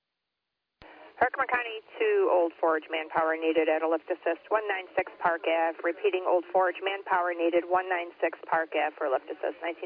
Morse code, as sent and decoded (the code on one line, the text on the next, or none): none